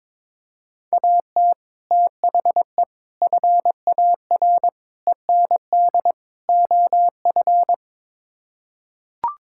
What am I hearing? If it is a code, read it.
Morse code, heard as a keyed tone, about 22 wpm, 700 Hz. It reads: AT THE FAR END OF
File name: practice / 22wpm / Build A Phrase - 5-Word Phrases - 22wpm